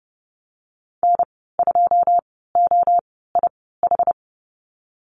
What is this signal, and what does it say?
Morse code, heard as a keyed tone, about 32 wpm, spaced as for 17 wpm, 700 Hz. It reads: N2OIH
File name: Lesson 19g - Letter w - Call Signs 30wpm